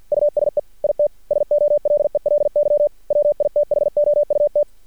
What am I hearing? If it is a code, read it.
Morse code, heard as a keyed tone, about 50 wpm, 590 Hz. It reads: USE IT SOLELY WITHOUT
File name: archived / CRSnet